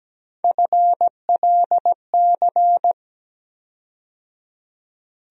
Morse code, heard as a keyed tone, about 17 wpm, 700 Hz. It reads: FLC
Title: USA Call Sign Postfixes - 3 Letters 17wpm